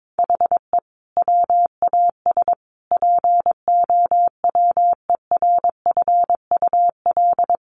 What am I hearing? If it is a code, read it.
Morse code, heard as a keyed tone, about 22 wpm, 700 Hz. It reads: HE WAS POWERFUL
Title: Sentences from Top 1300 Words 22wpm